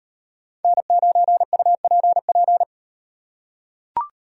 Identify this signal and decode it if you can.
Morse code, heard as a keyed tone, about 38 wpm, 700 Hz. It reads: N9UPP